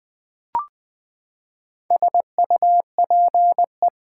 Morse code, heard as a keyed tone, about 20 wpm, 700 Hz. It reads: SUPE
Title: Build A Word - 4 Letters At A Time - 11-Letter Words - 20wpm